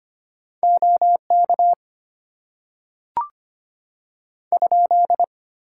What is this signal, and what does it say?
Morse code, heard as a keyed tone, about 25 wpm, 700 Hz. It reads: OK ?
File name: Top 90 QSO Elements - 8x Character Spacing - 25wpm